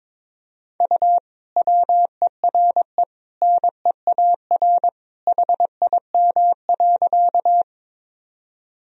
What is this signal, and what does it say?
Morse code, heard as a keyed tone, about 22 wpm, 700 Hz. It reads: U WERE NEAR HIM.
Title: Sentences from Top 200 Words and Abbreviations 22wpm